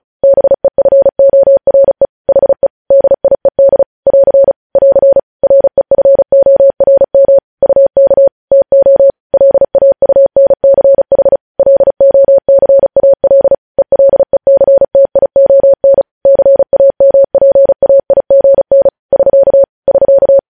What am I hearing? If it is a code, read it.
Morse code, heard as a keyed tone, about 36 wpm, 550 Hz. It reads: BEFORE HE DIED <AR> <AR> REFORM UK TO LAUNCH LOCAL ELECTION CAMPAIGN <SK> <SK>
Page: News headlines in Morse code at 35 wpm